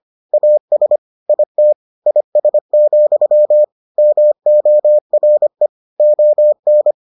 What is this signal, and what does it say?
Morse code, heard as a keyed tone, about 25 wpm, 600 Hz. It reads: AS IT IS, MORE ON